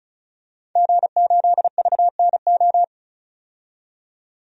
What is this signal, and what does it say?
Morse code, heard as a keyed tone, about 36 wpm, 700 Hz. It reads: G8VNO
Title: DXCC-100 Call Signs - 5 Characters - 35wpm